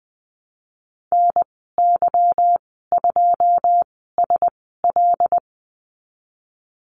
Morse code, heard as a keyed tone, about 20 wpm, spaced as for 15 wpm, 700 Hz. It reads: NY2SL
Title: 173 Lesson 28g - Number 9 - Call Signs 20wpm